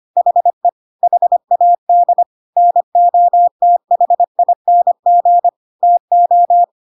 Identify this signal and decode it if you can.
Morse code, heard as a keyed tone, about 25 wpm, 700 Hz. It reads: HE HAD NOTHING TO
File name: Pinocchio / Pinocchio ch5 25 25 ik1hge